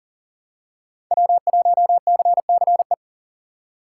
Morse code, heard as a keyed tone, about 39 wpm, 700 Hz. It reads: W1CCE
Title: K1USN SST QSO Practice - 40wpm